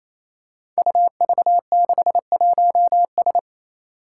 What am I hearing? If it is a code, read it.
Morse code, heard as a keyed tone, about 28 wpm, 700 Hz. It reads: UV61S
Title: Five Challenging Letter-Number Combinations 28wpm